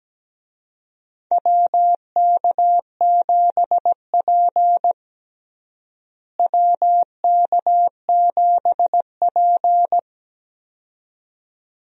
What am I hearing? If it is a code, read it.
Morse code, heard as a keyed tone, about 17 wpm, 700 Hz. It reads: WK7P WK7P
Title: USA Call Signs - 4 Characters - Repeated 2x - 17wpm